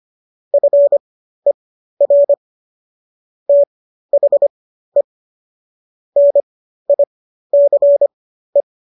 Morse code, heard as a keyed tone, about 25 wpm, spaced as for 13 wpm, 570 Hz. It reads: FER THE NICE